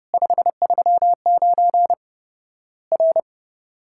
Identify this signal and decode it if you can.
Morse code, heard as a keyed tone, about 30 wpm, 700 Hz. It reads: 539 R
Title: Worked All Europe DX Contest - QTC Exchange - 30wpm